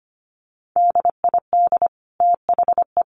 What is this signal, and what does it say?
Morse code, heard as a keyed tone, about 25 wpm, 700 Hz. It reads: DID THE
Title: Sentences from Top 800 Words 25wpm